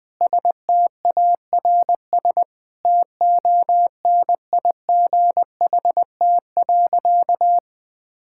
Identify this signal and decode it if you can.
Morse code, heard as a keyed tone, about 21 wpm, 700 Hz. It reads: STARS TONIGHT.
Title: Sentences with 8-Word Phrases - 20wpm